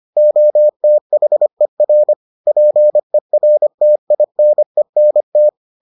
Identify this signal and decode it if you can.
Morse code, heard as a keyed tone, about 25 wpm, 600 Hz. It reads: OTHER PERTINENT